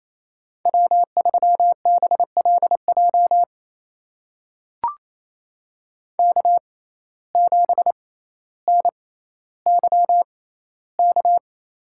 Morse code, heard as a keyed tone, about 28 wpm, spaced as for 14 wpm, 700 Hz. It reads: W3BLJ K7NYK
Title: USA Call Signs - 5 Characters - 6x Character Spacing - 28wpm